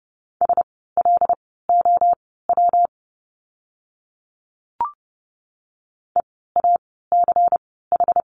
Morse code, heard as a keyed tone, about 30 wpm, spaced as for 17 wpm, 700 Hz. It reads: SLOW EACH
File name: course / 30:10wpm / 115 Lesson 19f - Letter w - Words 30wpm